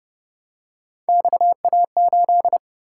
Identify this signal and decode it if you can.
Morse code, heard as a keyed tone, about 27 wpm, 700 Hz. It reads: XA8